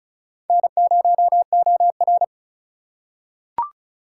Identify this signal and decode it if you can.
Morse code, heard as a keyed tone, about 36 wpm, 700 Hz. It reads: N0OR
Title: USA Call Signs - 3x Character Spacing - 35wpm